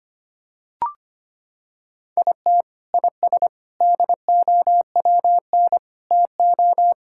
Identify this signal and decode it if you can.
Morse code, heard as a keyed tone, about 25 wpm, 700 Hz. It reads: IT IS DOWN TO
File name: practice / 25wpm / Sentences with 3-Word Phrases - 25wpm